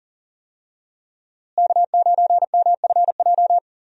Morse code, heard as a keyed tone, about 37 wpm, 700 Hz. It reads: K9MFJ
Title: USA Call Signs - 5 Characters - Repeated 3x - 40wpm